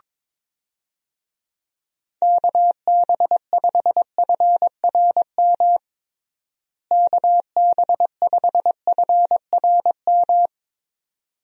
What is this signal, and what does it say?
Morse code, heard as a keyed tone, about 22 wpm, 700 Hz. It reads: KB5FRM KB5FRM
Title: USA Call Signs - 6 Characters - Repeated 2x - 22wpm